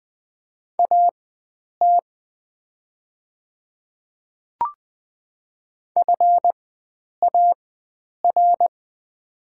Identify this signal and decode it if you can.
Morse code, heard as a keyed tone, about 20 wpm, spaced as for 9 wpm, 700 Hz. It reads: AT FAR
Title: 4x Character Spacing - Top 500 Words - 20wpm